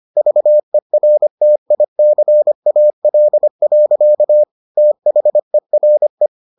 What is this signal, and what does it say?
Morse code, heard as a keyed tone, about 25 wpm, 600 Hz. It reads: VERTICAL. THERE